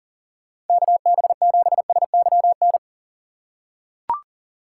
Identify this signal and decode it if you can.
Morse code, heard as a keyed tone, about 40 wpm, 700 Hz. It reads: KB7SYN
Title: USA Call Signs - 6 Characters - 4x Character Spacing - 40wpm